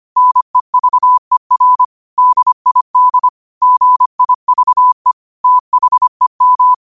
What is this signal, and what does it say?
Morse code, heard as a keyed tone, about 25 wpm, 1000 Hz. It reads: NEVER DID GIVE THEM
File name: qotdfiles / 25wpm